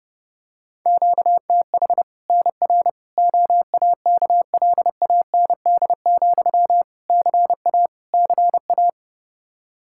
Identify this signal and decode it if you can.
Morse code, heard as a keyed tone, about 30 wpm, 700 Hz. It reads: QTH NR OAKLAND, CA CA
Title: K1USN SST QSO Practice 30wpm